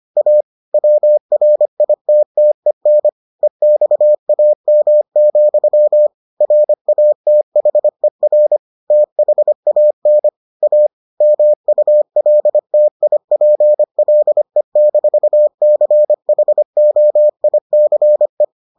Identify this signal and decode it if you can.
Morse code, heard as a keyed tone, about 25 wpm, 600 Hz. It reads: A WRITTEN EXAM, RATHER THAN A MULTIPLE-CHOICE